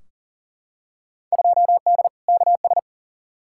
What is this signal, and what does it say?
Morse code, heard as a keyed tone, about 40 wpm, 700 Hz. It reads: 2D KS
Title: Field Day Exchanges Encore - 40wpm